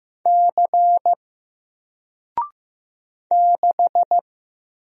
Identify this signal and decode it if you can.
Morse code, heard as a keyed tone, about 15 wpm, 700 Hz. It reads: C 6
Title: Single Letter-Number 15wpm